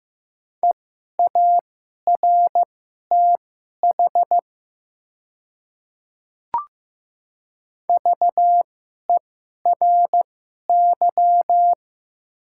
Morse code, heard as a keyed tone, about 15 wpm, spaced as for 11 wpm, 700 Hz. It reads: EARTH VERY
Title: 2x Character Spacing - Top 200 Words - 15wpm